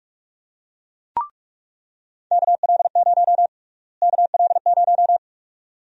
Morse code, heard as a keyed tone, about 47 wpm, 700 Hz. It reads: KL0 KL0
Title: USA Call Sign Prefixes - Repeated 2x - 45wpm